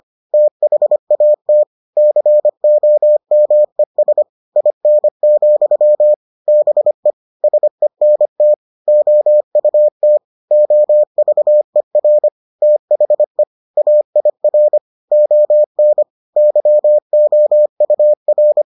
Morse code, heard as a keyed tone, about 25 wpm, 600 Hz. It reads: THAT COMES IN, BE SENT OUT OVER THE AIR ON YOUR